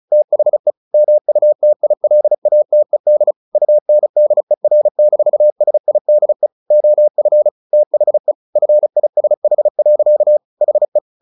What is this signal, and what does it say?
Morse code, heard as a keyed tone, about 34 wpm, 600 Hz. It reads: THE MUTILATED UNDER-SIDE OF THE FISH. HE
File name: the old man and the sea 2 35wpm